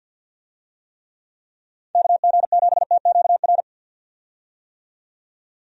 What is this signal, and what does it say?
Morse code, heard as a keyed tone, about 53 wpm, 700 Hz. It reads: KG7TQR
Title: USA Call Signs - 6 Characters - 50wpm